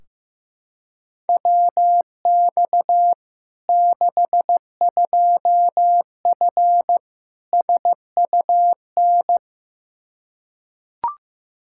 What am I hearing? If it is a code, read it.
Morse code, heard as a keyed tone, about 15 wpm, 700 Hz. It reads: WX 62F SUN
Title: Practice QSO - Exchange 2 - Info - 15wpm